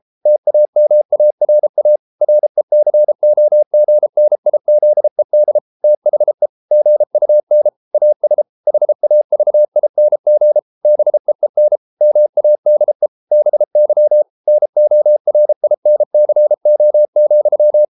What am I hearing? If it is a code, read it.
Morse code, heard as a keyed tone, about 33 wpm, 600 Hz. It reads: TAMARA RECOGNIZED THE GUN AS HAVING BEEN MADE BY NORINCO,